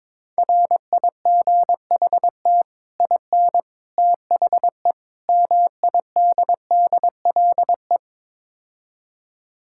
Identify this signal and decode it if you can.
Morse code, heard as a keyed tone, about 22 wpm, 700 Hz. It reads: RIGHT IN THE MIDDLE
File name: practice / 22wpm / Build A Phrase - 5-Word Phrases - 22wpm